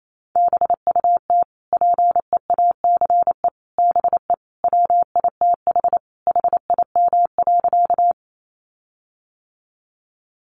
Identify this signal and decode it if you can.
Morse code, heard as a keyed tone, about 28 wpm, 700 Hz. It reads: BUT PEACE BE WITH HIM.
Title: Sentences from Top 1500 Words 28wpm